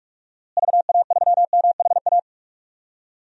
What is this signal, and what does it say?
Morse code, heard as a keyed tone, about 41 wpm, 700 Hz. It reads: VA3MHA
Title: DXCC-100 Call Signs - 6 Characters - Repeated 2x - 45wpm